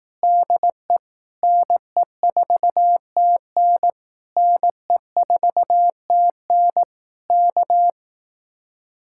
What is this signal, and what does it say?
Morse code, heard as a keyed tone, about 18 wpm, 700 Hz. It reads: DE NE4TN NE4TN K
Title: Speed-Racing Common QSO Phrases Encore - 15wpm